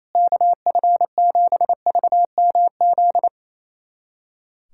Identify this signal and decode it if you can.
Morse code, heard as a keyed tone, about 27 wpm, 700 Hz. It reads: KF7VMZ
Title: USA Callsign DE Callsign - 6 Characters - 28wpm